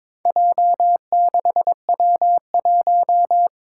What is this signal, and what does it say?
Morse code, heard as a keyed tone, about 22 wpm, 700 Hz. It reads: J6W1
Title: Four Challenging Letter-Number Combinations 22wpm